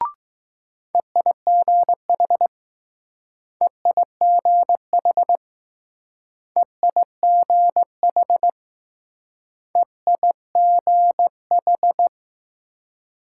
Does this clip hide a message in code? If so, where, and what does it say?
Yes: Morse code from 0.9 to 12.1 s, heard as a keyed tone, about 20 wpm, 700 Hz. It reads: EIGH EIGH EIGH EIGH